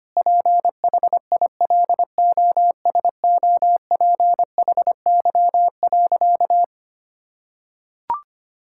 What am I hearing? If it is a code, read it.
Morse code, heard as a keyed tone, about 25 wpm, 700 Hz. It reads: PHILOSOPHY.
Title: Sentences from Top 1000 Words - Review Part 6 - 25wpm